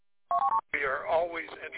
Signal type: none